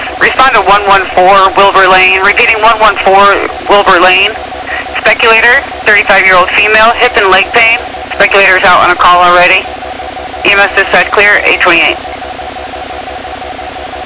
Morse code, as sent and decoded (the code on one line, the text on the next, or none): none